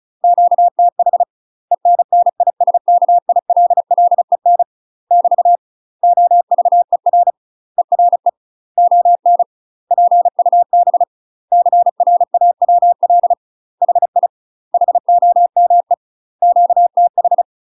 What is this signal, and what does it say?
Morse code, heard as a keyed tone, about 33 wpm, 700 Hz. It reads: QTH ENNISKILLEN <BT> OVER ERE ON PUB CRAWL HI HOME QTH
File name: UK MASTER QSOs TRAINING / 35WPM / UK MASTER QSOs 0017